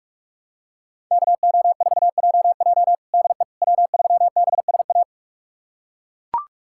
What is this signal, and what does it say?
Morse code, heard as a keyed tone, about 41 wpm, 700 Hz. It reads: KO4JJ DE W3BSA
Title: USA Callsign DE Callsign - 5 Characters - 45wpm